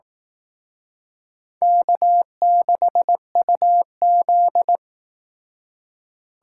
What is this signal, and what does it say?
Morse code, heard as a keyed tone, about 19 wpm, 700 Hz. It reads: K6UZ
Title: Speed Racing - USA 1x2 Call Signs 15wpm